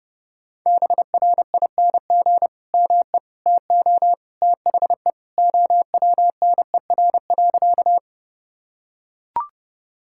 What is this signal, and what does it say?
Morse code, heard as a keyed tone, about 32 wpm, 700 Hz. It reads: BRING ME TO THE OWNER.
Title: Sentences from Top 1600 Words 30wpm